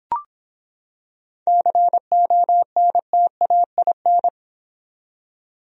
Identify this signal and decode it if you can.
Morse code, heard as a keyed tone, about 26 wpm, 700 Hz. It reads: CONTAIN